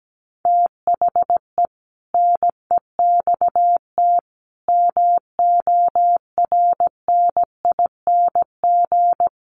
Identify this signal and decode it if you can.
Morse code, heard as a keyed tone, about 17 wpm, 700 Hz. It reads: THE NEXT MORNING
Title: Sentences from Top 600 Words 17wpm